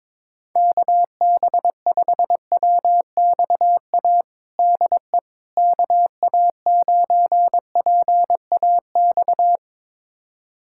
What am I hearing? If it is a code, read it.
Morse code, heard as a keyed tone, about 22 wpm, 700 Hz. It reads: KB5WXA DE KA9PAX
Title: USA Callsign DE Callsign - 6 Characters - 22wpm